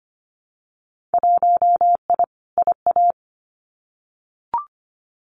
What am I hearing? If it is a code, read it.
Morse code, heard as a keyed tone, about 25 wpm, 700 Hz. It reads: 1I IA
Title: Winter Field Day Exchanges 25wpm